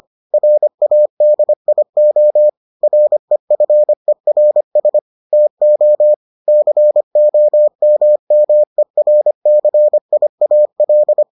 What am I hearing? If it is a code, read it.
Morse code, heard as a keyed tone, about 25 wpm, 600 Hz. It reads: RADIO REFERS TO COMMERCIAL